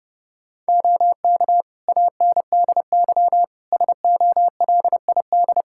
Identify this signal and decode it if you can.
Morse code, heard as a keyed tone, about 30 wpm, 700 Hz. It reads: OK ANDY SOLID
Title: K1USN SST QSO Practice 30wpm